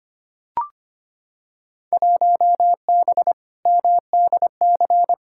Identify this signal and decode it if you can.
Morse code, heard as a keyed tone, about 25 wpm, 700 Hz. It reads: 1B MDC
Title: Field Day Exchanges Encore - 25wpm